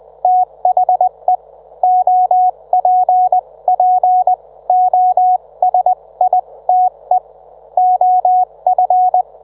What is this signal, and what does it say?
Morse code, heard as a keyed tone, about 20 wpm, spaced as for 17 wpm, 730 Hz. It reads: THE OPPOSITE OF